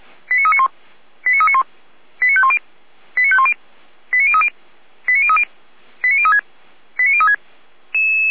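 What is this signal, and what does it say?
Five-tone selective call (ZVEI), heard as quick runs of five tones, tones of about 70 ms. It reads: 89391 89391 89510 89510 89030 89030 89036 89036
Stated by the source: Five-tone ZVEI selective call received by SDR.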